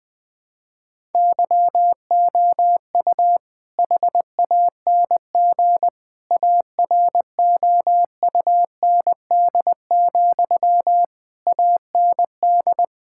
Morse code, heard as a keyed tone, about 21 wpm, 700 Hz. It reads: YOU HANG AROUND, AND
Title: All-Age Appropriate Jokes - 20wpm